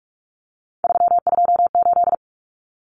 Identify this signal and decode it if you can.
Morse code, heard as a keyed tone, about 44 wpm, 700 Hz. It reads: 328
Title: Common Cut Numbers - 3-Digit 45wpm